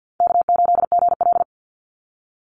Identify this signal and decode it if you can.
Morse code, heard as a keyed tone, about 50 wpm, 700 Hz. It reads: K8GL
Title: Call Signs - 50wpm